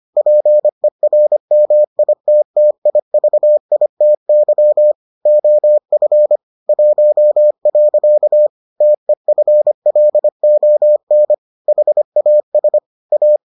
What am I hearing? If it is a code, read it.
Morse code, heard as a keyed tone, about 25 wpm, 600 Hz. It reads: PERMITTIVITY OF 1. TEFLON HAS A